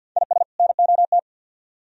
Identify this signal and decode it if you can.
Morse code, heard as a keyed tone, about 58 wpm, 690 Hz. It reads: IS NOT
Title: Speed-Racing - Sets of 2 Words 50wpm